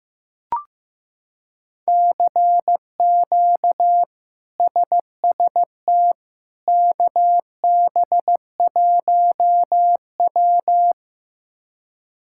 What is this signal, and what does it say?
Morse code, heard as a keyed tone, about 15 wpm, 700 Hz. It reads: CQ SST KB1W
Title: K1USN SST QSO Practice - 15wpm